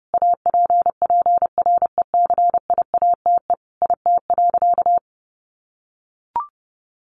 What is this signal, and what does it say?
Morse code, heard as a keyed tone, about 30 wpm, 700 Hz. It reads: APPRECIATE IT.
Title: Sentences from Top 1300 Words 30wpm